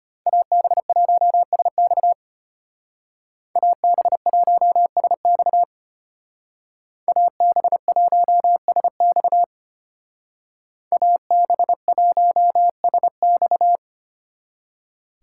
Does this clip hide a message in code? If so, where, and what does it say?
Yes: Morse code from 0.3 to 13.8 s, heard as a keyed tone, about 33 wpm, 700 Hz. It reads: AB1SX AB1SX AB1SX AB1SX